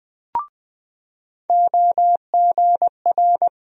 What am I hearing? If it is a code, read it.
Morse code, heard as a keyed tone, about 20 wpm, 700 Hz. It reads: OGR
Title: USA Call Sign Postfixes - 3 Letters 20wpm